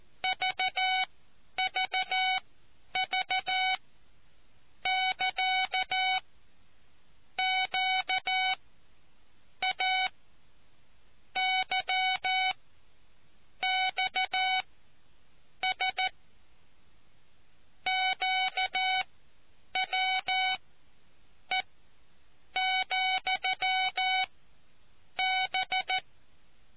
Morse code, heard as a keyed tone, about 14 wpm, spaced as for 7 wpm, 750 Hz. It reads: VVV <KA> QA YXS QWE,B